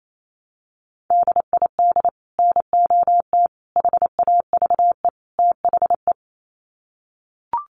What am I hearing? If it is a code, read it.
Morse code, heard as a keyed tone, about 28 wpm, 700 Hz. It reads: DID NOT HAVE THE
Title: Sets of 4 Words 28wpm